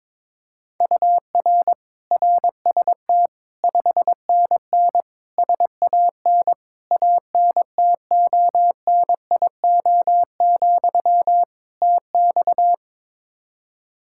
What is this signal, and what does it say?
Morse code, heard as a keyed tone, about 22 wpm, 700 Hz. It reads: UR RST 5NN SAN ANTONIO, TX
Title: Practice QSO - Exchange 1 - Beginning - 22wpm